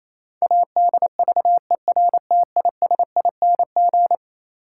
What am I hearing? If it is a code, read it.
Morse code, heard as a keyed tone, about 28 wpm, 700 Hz. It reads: ADVERTISING
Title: Sentences from Top 1000 Words - Review Part 6 - New Word Drill - 28wpm